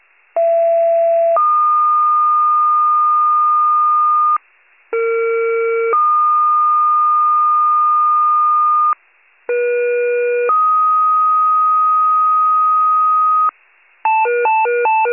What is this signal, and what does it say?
Two-tone page, heard as a one-second tone, then a three-second tone, about 670 then 1220 Hz, about 460 then 1220 Hz, about 480 then 1250 Hz. It is heard from 0.4 to 13.5 s.